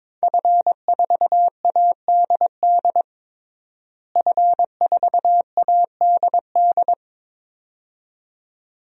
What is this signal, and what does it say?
Morse code, heard as a keyed tone, about 21 wpm, 700 Hz. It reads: F4ADD F4ADD
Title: DXCC-100 Call Signs - 5 Characters - Repeated 2x - 22wpm